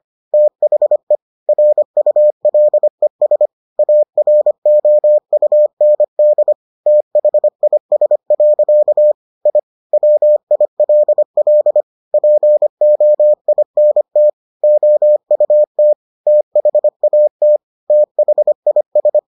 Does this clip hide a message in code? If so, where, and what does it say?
Yes: Morse code from 0.3 to 19.2 s, heard as a keyed tone, about 25 wpm, 600 Hz. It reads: THE RULES AROUND THIS. I WILL POINT OUT THAT THIS